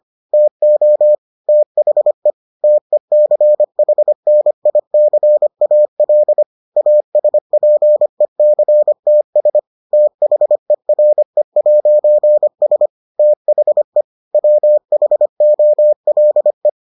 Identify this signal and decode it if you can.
Morse code, heard as a keyed tone, about 25 wpm, 600 Hz. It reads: TO THE TECHNICAL ASPECTS THERE'S THE WHOLE